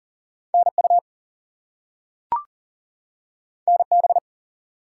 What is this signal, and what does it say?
Morse code, heard as a keyed tone, about 40 wpm, 700 Hz. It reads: NU NB
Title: US States and Canadian Provinces 40wpm